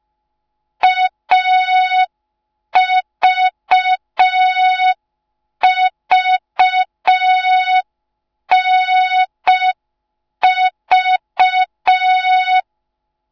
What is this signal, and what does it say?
Morse code, heard as a keyed tone, about 5 wpm, 3700 Hz. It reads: AVVNV